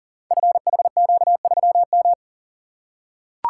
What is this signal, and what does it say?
Morse code, heard as a keyed tone, about 40 wpm, 700 Hz. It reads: FHQ3M
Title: Code Groups - 5 Letter:Numbers - 40wpm